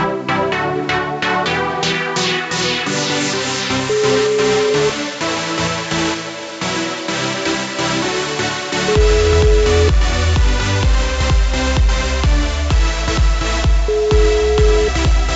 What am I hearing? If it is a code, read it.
Ringback tone (Europe, 425 Hz), with noise.